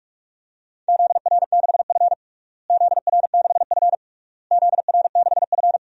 Morse code, heard as a keyed tone, about 46 wpm, 700 Hz. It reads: ZR6F ZR6F ZR6F
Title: DXCC-100 Call Signs - 4 Characters - Repeated 3x - 45wpm